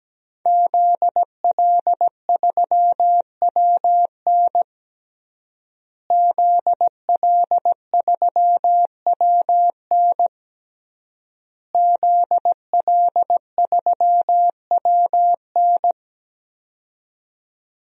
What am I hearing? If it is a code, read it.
Morse code, heard as a keyed tone, about 17 wpm, 700 Hz. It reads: ZL3WN ZL3WN ZL3WN
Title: DXCC-100 Call Signs - 5 Characters - Repeated 3x - 17wpm